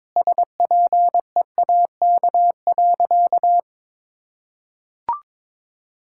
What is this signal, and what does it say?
Morse code, heard as a keyed tone, about 22 wpm, 700 Hz. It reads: SPEAK.
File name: practice / 22wpm / Sentences with 2-Word Phrases - 22wpm